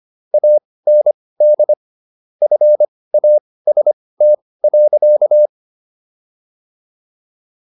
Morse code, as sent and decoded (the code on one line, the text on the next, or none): .- -. -.. / ..-. .- ... - .-.-.-
AND FAST.